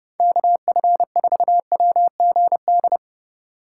Morse code, heard as a keyed tone, about 31 wpm, 700 Hz. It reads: KF4WGD